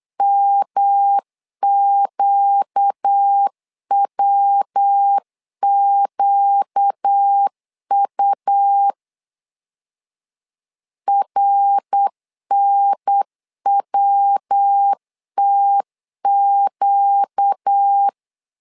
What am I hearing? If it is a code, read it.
Morse code, heard as a keyed tone, about 8 wpm, 800 Hz. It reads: MQWQU RNWTQ
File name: 4 Q 8wpm